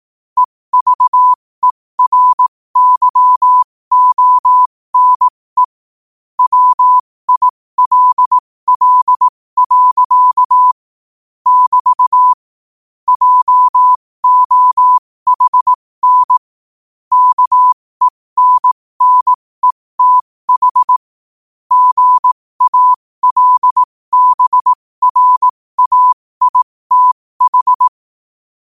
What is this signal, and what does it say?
Morse code, heard as a keyed tone, about 18 wpm, spaced as for 15 wpm, 1000 Hz. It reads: EVERYONE WILL. <BT> JOHN KENNETH GALBRAITH
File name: qotdfiles / 15wpm